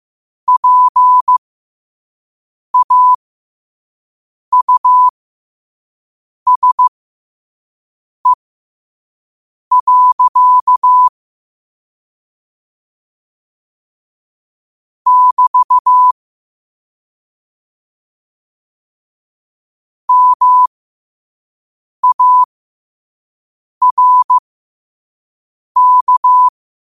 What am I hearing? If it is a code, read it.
Morse code, heard as a keyed tone, about 15 wpm, spaced as for 5 wpm, 1000 Hz. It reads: PAUSE. <BT> MARK
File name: qotdfiles / 5wpm